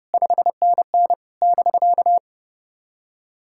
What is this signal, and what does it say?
Morse code, heard as a keyed tone, about 30 wpm, 700 Hz. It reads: 5NN <BK>
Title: SOTA QSO Practice 30wpm